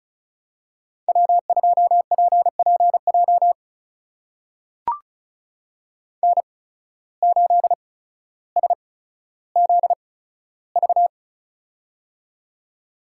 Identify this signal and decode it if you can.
Morse code, heard as a keyed tone, about 36 wpm, spaced as for 15 wpm, 700 Hz. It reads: W2PPJ N8SZV
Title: USA Call Signs - 5 Characters - 8x Character Spacing - 35wpm